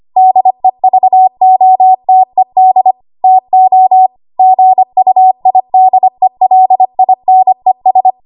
Morse code, heard as a keyed tone, about 25 wpm, 750 Hz. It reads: DEVOTED TO GUIDELINES
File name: Archive / 25 WPM / 25WPM